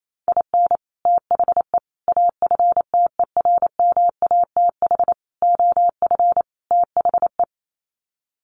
Morse code, heard as a keyed tone, about 28 wpm, 700 Hz. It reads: IN THE AFTERMATH OF THE
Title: Sets of 5 Words 28wpm